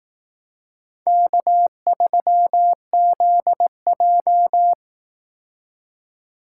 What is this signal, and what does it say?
Morse code, heard as a keyed tone, about 18 wpm, 700 Hz. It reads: K3ZJ
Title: Speed Racing - USA 1x2 Call Signs 15wpm